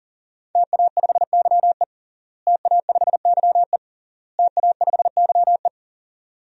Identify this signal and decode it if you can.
Morse code, heard as a keyed tone, about 40 wpm, 700 Hz. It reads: TA5YE TA5YE TA5YE
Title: DXCC-100 Call Signs - 5 Characters - Repeated 3x - 40wpm